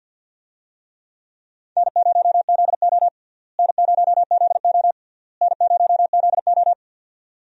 Morse code, heard as a keyed tone, about 50 wpm, 700 Hz. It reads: N0ZO N0ZO N0ZO
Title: USA Call Signs - 4 Characters - Repeated 3x - 50wpm